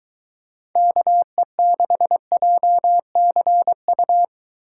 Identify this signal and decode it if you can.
Morse code, heard as a keyed tone, about 23 wpm, 700 Hz. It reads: KE6JCU